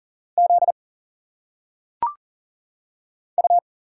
Morse code, heard as a keyed tone, about 40 wpm, 700 Hz. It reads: Z U
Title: Single Letters - Phonetic Pronunciation 40wpm